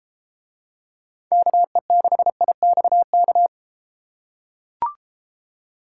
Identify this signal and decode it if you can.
Morse code, heard as a keyed tone, about 30 wpm, 700 Hz. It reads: KE6IXK